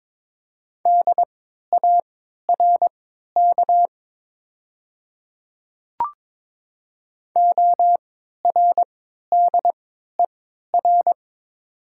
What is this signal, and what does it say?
Morse code, heard as a keyed tone, about 22 wpm, spaced as for 13 wpm, 700 Hz. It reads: DARK ORDER
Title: 3x Character Spacing - Top 500 Words - 22wpm